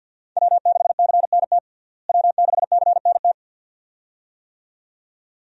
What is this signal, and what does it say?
Morse code, heard as a keyed tone, about 53 wpm, 700 Hz. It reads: W6CNT W6CNT
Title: USA Call Signs - 5 Characters - Repeated 2x - 50wpm